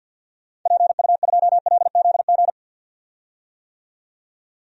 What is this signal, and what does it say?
Morse code, heard as a keyed tone, about 51 wpm, 700 Hz. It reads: PU2LZG
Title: DXCC-100 Call Signs - 6 Characters - 50wpm